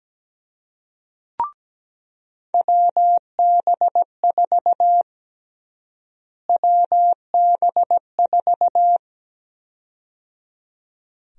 Morse code, heard as a keyed tone, about 17 wpm, 700 Hz. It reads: WB4 WB4
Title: USA Call Sign Prefixes - Repeated 2x - 17wpm